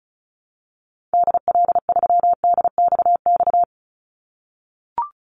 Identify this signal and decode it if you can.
Morse code, heard as a keyed tone, about 34 wpm, 700 Hz. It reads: DL3DXX
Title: Call Signs - 35wpm